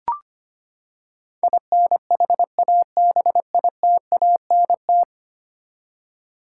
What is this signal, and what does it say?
Morse code, heard as a keyed tone, about 25 wpm, 700 Hz. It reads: INHABITANT
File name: Ten Letter Words - 25wpm